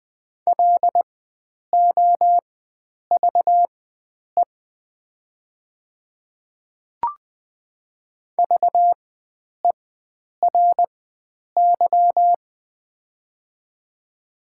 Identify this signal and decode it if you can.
Morse code, heard as a keyed tone, about 21 wpm, spaced as for 9 wpm, 700 Hz. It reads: LOVE VERY
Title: 4x Character Spacing - Top 500 Words - 20wpm